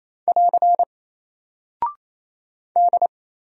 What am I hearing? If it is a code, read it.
Morse code, heard as a keyed tone, about 27 wpm, 700 Hz. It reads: <AR> D
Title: Single Character (Letter:Number:Punctuation:Prosign) 28wpm